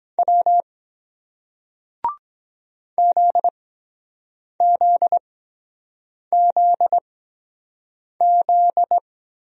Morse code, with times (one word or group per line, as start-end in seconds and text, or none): 0.2-0.6 W
3.0-9.0 ZZZZ